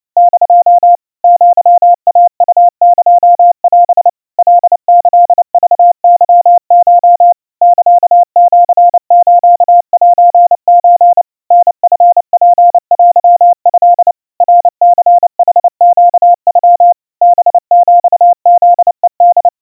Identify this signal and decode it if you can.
Morse code (Japanese Wabun code), heard as a keyed tone, about 29 wpm, 700 Hz. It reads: <DO> アイウエオ カキクケコ サシスセソ タチツテト ナニヌネノ ハヒフヘホ
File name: aiueo 100 cpm 700 Hz Crispy